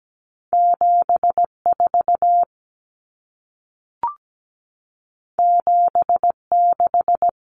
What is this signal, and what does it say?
Morse code, heard as a keyed tone, about 17 wpm, 700 Hz. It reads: 74 76
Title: Common Cut Numbers - 2-Digit 17wpm